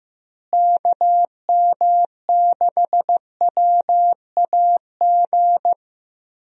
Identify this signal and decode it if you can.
Morse code, heard as a keyed tone, about 15 wpm, 700 Hz. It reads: KM6WAG
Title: USA Call Signs - 6 Characters - Repeated 3x - 15wpm